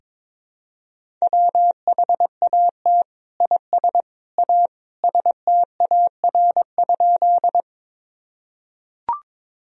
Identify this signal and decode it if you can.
Morse code, heard as a keyed tone, about 22 wpm, 700 Hz. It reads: WHAT IS A STAR?
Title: Sentences from Top 1000 Words - Review Part 2 - 22wpm